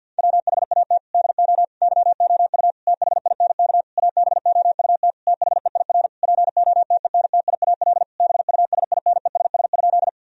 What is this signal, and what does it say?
Morse code, heard as a keyed tone, about 50 wpm, 700 Hz. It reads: WHAT DO YOU THINK ABOUT THEIR POTENTIAL BUSINESS?
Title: Sentences from Top 1000 Words - Review Part 4 - 50wpm